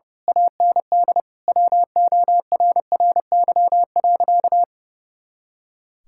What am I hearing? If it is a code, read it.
Morse code, heard as a keyed tone, about 32 wpm, 700 Hz. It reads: AND WORRY.
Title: Pangram Challenge v2 - 30wpm